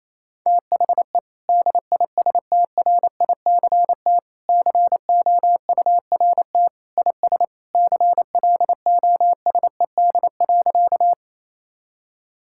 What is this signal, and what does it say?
Morse code, heard as a keyed tone, about 28 wpm, 700 Hz. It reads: THE DISTRICT COURT IS CLOSED.
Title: Sentences from Top 1600 Words 28wpm